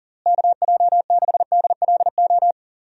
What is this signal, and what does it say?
Morse code, heard as a keyed tone, about 40 wpm, 700 Hz. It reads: KJ6DLO
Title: USA Call Signs - 6 Characters - Repeated 3x - 40wpm